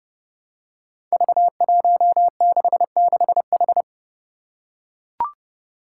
Morse code, heard as a keyed tone, about 30 wpm, 700 Hz. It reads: V166H
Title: Five Challenging Letter-Number Combinations 30wpm